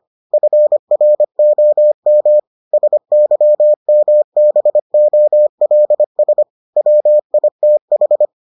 Morse code, heard as a keyed tone, about 25 wpm, 600 Hz. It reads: FROM SYMBOLS WITH